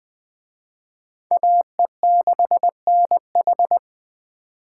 Morse code, heard as a keyed tone, about 20 wpm, 700 Hz. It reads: AE6NH
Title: Speed Racing - USA Call Signs 17wpm